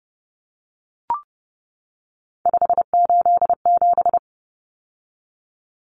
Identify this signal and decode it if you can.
Morse code, heard as a keyed tone, about 30 wpm, 700 Hz. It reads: 587
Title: Numbers - 3-Digit 30wpm